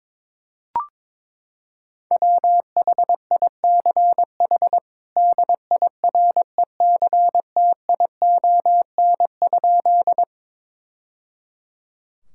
Morse code, heard as a keyed tone, about 22 wpm, 700 Hz. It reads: WHICH DIRECTION?
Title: Sentences from Top 1000 Words - Review Part 5 - 22wpm